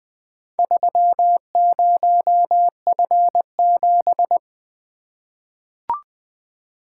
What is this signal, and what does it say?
Morse code, heard as a keyed tone, about 20 wpm, 700 Hz. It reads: 30F7
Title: Four Letter-Number Combinations - 20wpm